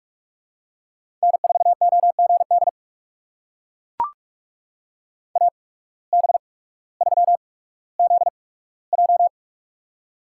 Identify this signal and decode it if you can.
Morse code, heard as a keyed tone, about 44 wpm, spaced as for 19 wpm, 700 Hz. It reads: N4OGD AB3ZJ
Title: USA Call Signs - 8x Character Spacing - 45wpm